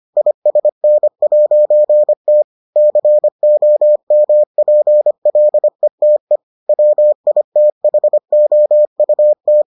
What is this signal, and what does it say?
Morse code, heard as a keyed tone, about 25 wpm, 600 Hz. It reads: ISN'T COMPLETE WITHOUT